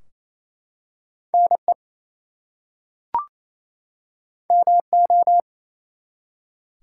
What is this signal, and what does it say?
Morse code, heard as a keyed tone, about 28 wpm, 700 Hz. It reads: NE MO
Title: US States and Canadian Provinces 28wpm